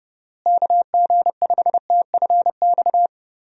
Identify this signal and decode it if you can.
Morse code, heard as a keyed tone, about 32 wpm, 700 Hz. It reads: KG5TFX
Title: USA Call Signs - 6 Characters - Repeated 2x - 30wpm